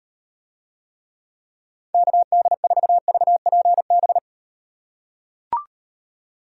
Morse code, heard as a keyed tone, about 38 wpm, 700 Hz. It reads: KD4VPB